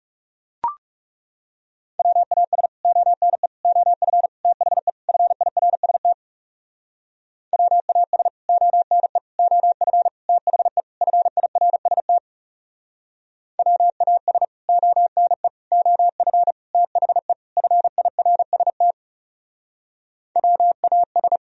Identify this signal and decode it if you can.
Morse code, heard as a keyed tone, about 37 wpm, 700 Hz. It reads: WAS ONE OF THE FIRST WAS ONE OF THE FIRST WAS ONE OF THE FIRST WAS
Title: Speed-Racing - Sets of 5 Words 30wpm